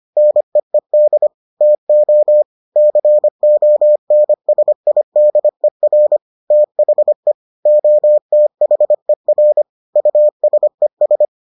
Morse code, heard as a keyed tone, about 25 wpm, 600 Hz. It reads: NEED TO CONSIDER THE OTHER USES